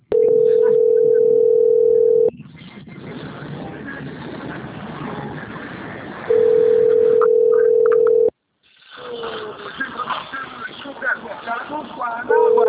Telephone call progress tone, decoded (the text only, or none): ringback tone